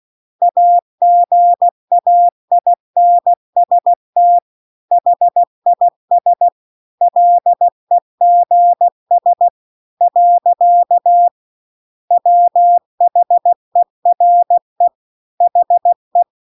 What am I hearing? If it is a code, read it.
Morse code, heard as a keyed tone, about 16 wpm, 700 Hz. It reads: AGAINST HIS LEGS . WHERE HE
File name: CW-The-Big-Twoheaded-River-2-16WPM